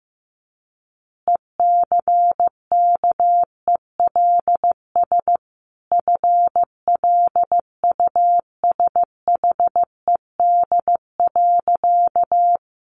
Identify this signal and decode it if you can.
Morse code, heard as a keyed tone, about 15 wpm, 700 Hz. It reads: ECKELS FLUSHED.
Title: A Sound Of Thunder 15wpm